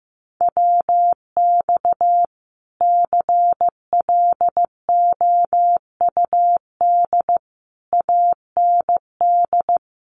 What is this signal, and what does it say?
Morse code, heard as a keyed tone, about 15 wpm, 700 Hz. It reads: WX CLOUD AND
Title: Common QSO Phrases Encore 15wpm